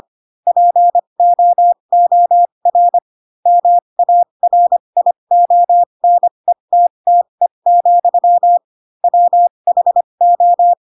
Morse code, heard as a keyed tone, about 25 wpm, spaced as for 22 wpm, 700 Hz. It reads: POOR MARIONETTE, WHO
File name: Pinocchio / Pinocchio ch7 18 25 ik1hge